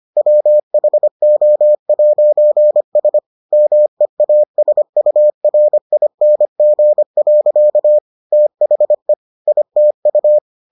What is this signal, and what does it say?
Morse code, heard as a keyed tone, about 25 wpm, 600 Hz. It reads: WHO'S MEASURING. THE ITU